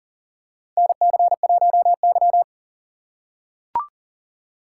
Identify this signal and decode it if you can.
Morse code, heard as a keyed tone, about 40 wpm, 700 Hz. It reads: NC1Y
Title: USA Call Signs - 4 Characters - 6x Character Spacing - 40wpm